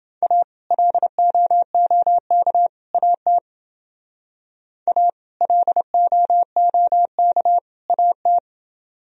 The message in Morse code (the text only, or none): A LOOK AT A LOOK AT